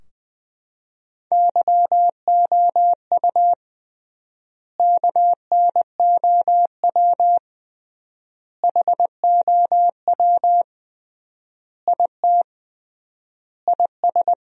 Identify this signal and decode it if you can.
Morse code, heard as a keyed tone, about 20 wpm, 700 Hz. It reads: YOU KNOW HOW IT IS
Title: 3x Word-Spacing - Sentences from Top 100 Words - 20wpm